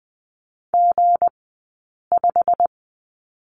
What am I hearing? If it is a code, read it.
Morse code, heard as a keyed tone, about 20 wpm, 700 Hz. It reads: G 5
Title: 190 Lesson 31d - Letter v - Two Characters 20wpm